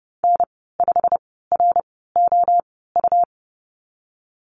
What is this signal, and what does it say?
Morse code, heard as a keyed tone, about 30 wpm, spaced as for 17 wpm, 700 Hz. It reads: N5ROU
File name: Lesson 19g - Letter w - Call Signs 30wpm